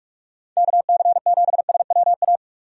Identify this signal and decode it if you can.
Morse code, heard as a keyed tone, about 41 wpm, 700 Hz. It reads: KC7SWA
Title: USA Call Signs - 6 Characters - 45wpm